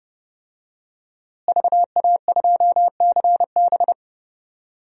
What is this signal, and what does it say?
Morse code, heard as a keyed tone, about 30 wpm, 700 Hz. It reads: VA2CB
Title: DXCC-100 Call Signs - 5 Characters - Repeated 2x - 30wpm